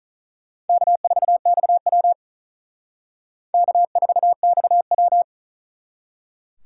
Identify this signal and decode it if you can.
Morse code, heard as a keyed tone, about 43 wpm, 700 Hz. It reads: K4XW K4XW
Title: Speed Racing - USA 1x2 Call Signs 35wpm